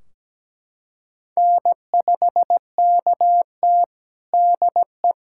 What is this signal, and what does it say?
Morse code, heard as a keyed tone, about 17 wpm, 700 Hz. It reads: N5KT DE
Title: Practice QSO - Exchange 4 - Ending - 17wpm